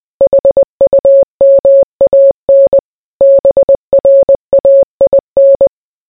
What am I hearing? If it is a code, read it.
Morse code, heard as a keyed tone, about 20 wpm, 550 Hz. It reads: HUMAN BRAIN